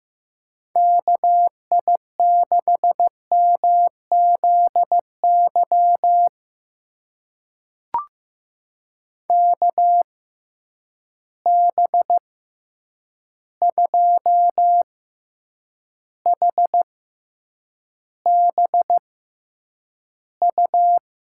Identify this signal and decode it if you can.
Morse code, heard as a keyed tone, about 15 wpm, spaced as for 8 wpm, 700 Hz. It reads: KI6MZY KB2HBU
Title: USA Call Signs - 6 Characters - 6x Character Spacing - 15wpm